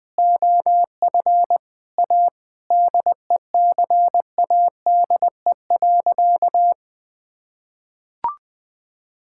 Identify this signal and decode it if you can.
Morse code, heard as a keyed tone, about 20 wpm, 700 Hz. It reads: OF A DECADE.
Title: Sentences with 7-Word Phrases - Spoken Beginning - 20wpm